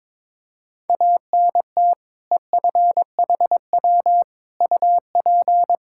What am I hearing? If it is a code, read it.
Morse code, heard as a keyed tone, about 22 wpm, 700 Hz. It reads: ANT EFHW UP
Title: Practice QSO - Exchange 2 - Info - 22wpm